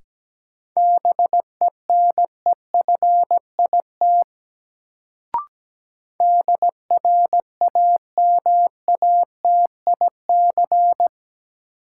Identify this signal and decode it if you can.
Morse code, heard as a keyed tone, about 17 wpm, 700 Hz. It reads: BENEFIT DRAMATIC
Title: Sentences from Top 1300 Words - New Word Drill Alt - 17wpm